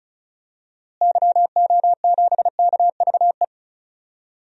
Morse code, heard as a keyed tone, about 35 wpm, 700 Hz. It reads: YO7KVE
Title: DXCC-100 Call Signs - 6 Characters - Repeated 3x - 35wpm